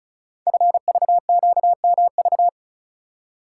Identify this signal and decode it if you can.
Morse code, heard as a keyed tone, about 36 wpm, 700 Hz. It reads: FVQMV
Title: Code Groups - 5 Letter:Numbers - 35wpm